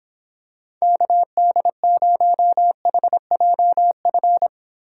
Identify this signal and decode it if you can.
Morse code, heard as a keyed tone, about 26 wpm, 700 Hz. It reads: KD0HJF